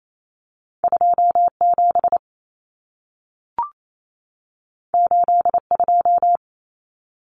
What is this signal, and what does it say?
Morse code, heard as a keyed tone, about 28 wpm, 700 Hz. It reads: 27 82
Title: Numbers - 2-Digit 28wpm